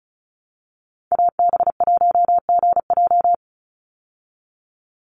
Morse code, heard as a keyed tone, about 34 wpm, 700 Hz. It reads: AB1GJ
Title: Call Signs Repeated 3x - 35wpm